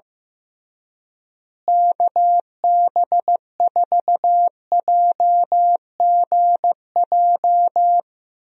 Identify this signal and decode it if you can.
Morse code, heard as a keyed tone, about 15 wpm, 700 Hz. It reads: KB4JGJ
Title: USA Call Signs - 6 Characters - Repeated 3x - 15wpm